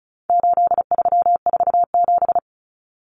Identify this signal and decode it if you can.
Morse code, heard as a keyed tone, about 34 wpm, 700 Hz. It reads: 8347
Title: Numbers - 4-Digit 35wpm